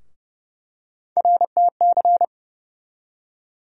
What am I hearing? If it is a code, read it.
Morse code, heard as a keyed tone, about 31 wpm, 700 Hz. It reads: RTC
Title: USA Call Sign Postfixes - 3 Letters 30wpm